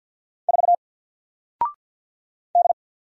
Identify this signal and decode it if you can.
Morse code, heard as a keyed tone, about 49 wpm, 700 Hz. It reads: 4 D
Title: Single Letter-Number Alt 50wpm